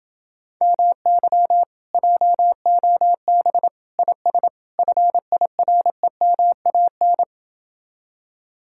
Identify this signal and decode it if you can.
Morse code, heard as a keyed tone, about 30 wpm, 700 Hz. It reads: MY JOB IS FIREMAN